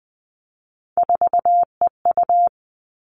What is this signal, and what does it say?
Morse code, heard as a keyed tone, about 20 wpm, 700 Hz. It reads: 4EU
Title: All Cut Numbers - 3-Digit 20wpm